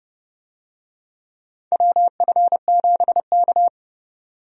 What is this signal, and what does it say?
Morse code, heard as a keyed tone, about 32 wpm, 700 Hz. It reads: WF7K
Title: USA Call Signs - 4 Characters - 30wpm